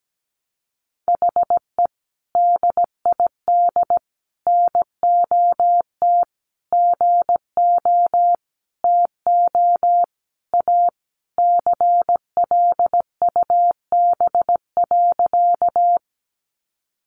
Morse code, heard as a keyed tone, about 17 wpm, 700 Hz. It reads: HE DID NOT GO TO A CLUB.
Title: Sentences from Top 1200 Words 17wpm